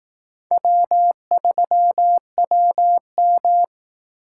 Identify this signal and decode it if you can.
Morse code, heard as a keyed tone, about 18 wpm, 700 Hz. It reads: W3WM
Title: Speed Racing - USA 1x2 Call Signs 15wpm